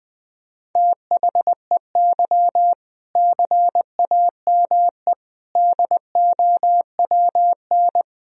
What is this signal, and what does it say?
Morse code, heard as a keyed tone, about 21 wpm, 700 Hz. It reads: THEY CAME DOWN
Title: Sentences from Top 1000 Words - Review Part 1 - 20wpm